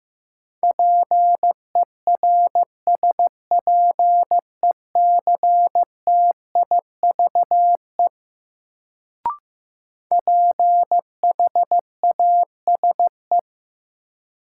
Morse code, heard as a keyed tone, about 15 wpm, 700 Hz. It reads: PERSPECTIVE PHASE
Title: Sentences from Top 1200 Words - New Word Drill Alt - 15wpm